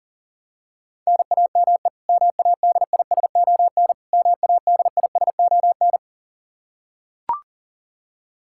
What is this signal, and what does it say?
Morse code, heard as a keyed tone, about 40 wpm, 700 Hz. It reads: NAME MADISON MADISON
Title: K1USN SST QSO Practice 40wpm